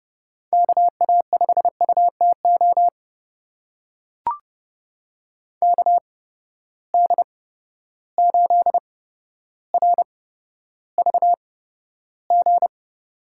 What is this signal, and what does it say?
Morse code, heard as a keyed tone, about 30 wpm, spaced as for 13 wpm, 700 Hz. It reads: KA5UTO KD8RVG
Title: USA Call Signs - 6 Characters - 8x Character Spacing - 30wpm